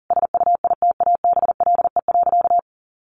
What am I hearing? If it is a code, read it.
Morse code, heard as a keyed tone, about 40 wpm, 700 Hz. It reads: SUITABLE.
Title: Sentences from Top 1400 Words 40wpm